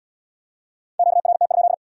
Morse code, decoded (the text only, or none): QRT?